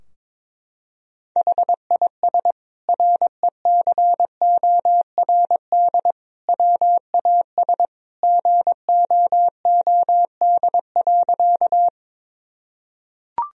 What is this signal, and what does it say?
Morse code, heard as a keyed tone, about 22 wpm, 700 Hz. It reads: HIS RECORD WAS GOOD.
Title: Sentences from Top 1000 Words - Review Part 3 - 22wpm